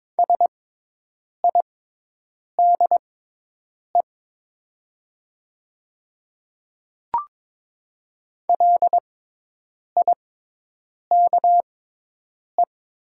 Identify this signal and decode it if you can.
Morse code, heard as a keyed tone, about 22 wpm, spaced as for 8 wpm, 700 Hz. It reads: SIDE LIKE
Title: 6x Character Spacing - Top 100 Words - 22wpm